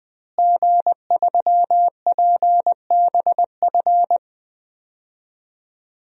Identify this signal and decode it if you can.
Morse code, heard as a keyed tone, about 20 wpm, 700 Hz. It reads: G3PBF
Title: DXCC-100 Call Signs - 5 Characters - Repeated 3x - 20wpm